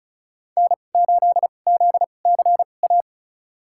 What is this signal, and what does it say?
Morse code, heard as a keyed tone, about 36 wpm, spaced as for 25 wpm, 700 Hz. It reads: N8ZCA
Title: USA Call Signs - 5 Characters - 2x Character Spacing - 35wpm